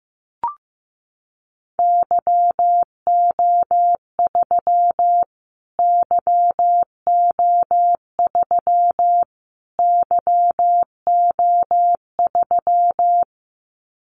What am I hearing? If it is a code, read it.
Morse code, heard as a keyed tone, about 15 wpm, 700 Hz. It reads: YO3 YO3 YO3
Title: Callsign Prefixes for Top 25 DXCC - Repeated 3x - 15wpm